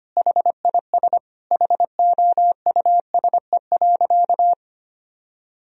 Morse code, heard as a keyed tone, about 25 wpm, 700 Hz. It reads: HIS HOUSE.
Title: Sentences from Top 1000 Words - Review Part 2 - 25wpm